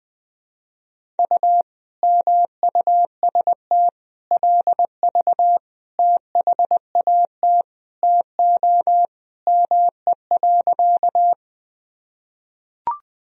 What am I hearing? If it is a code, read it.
Morse code, heard as a keyed tone, about 21 wpm, 700 Hz. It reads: U MUST LV THAT TO ME.
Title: Sentences from Top 400 Words and Abbreviations 20wpm